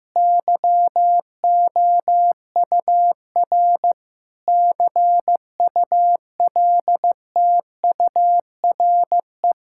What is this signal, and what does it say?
Morse code, heard as a keyed tone, about 14 wpm, 700 Hz. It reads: YOUR CULTURE
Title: Sentences from Top 1000 Words - Review Part 6 - 15wpm